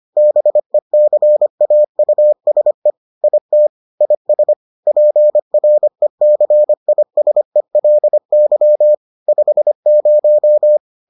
Morse code, heard as a keyed tone, about 25 wpm, 600 Hz. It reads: BECAUSE IT IS PRECISELY 50